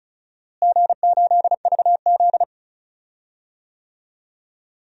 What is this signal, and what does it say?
Morse code, heard as a keyed tone, about 36 wpm, 700 Hz. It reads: G8VZ
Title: DXCC-100 Call Signs - 4 Characters - Repeated 3x - 35wpm